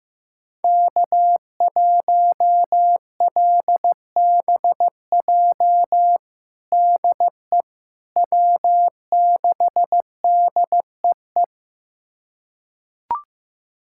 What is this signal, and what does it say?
Morse code, heard as a keyed tone, about 15 wpm, 700 Hz. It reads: K1LBJ DE W6DEE
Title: USA Callsign DE Callsign - 5 Characters - 15wpm